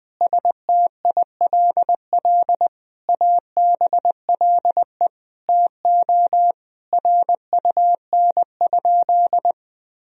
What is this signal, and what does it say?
Morse code, heard as a keyed tone, about 20 wpm, 700 Hz. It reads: STILL ABLE TO RUN?
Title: Sentences from Top 1000 Words - Review Part 2 - 20wpm